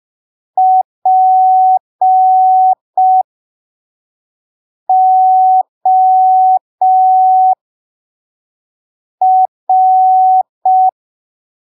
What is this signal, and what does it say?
Morse code, heard as a keyed tone, about 5 wpm, spaced as for 3 wpm, 750 Hz. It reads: POR
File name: arquivos / texto / texto 20 05ppm